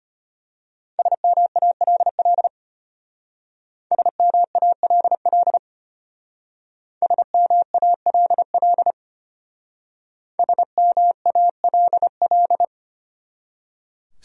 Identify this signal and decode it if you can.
Morse code, heard as a keyed tone, about 32 wpm, 700 Hz. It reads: SMALL SMALL SMALL SMALL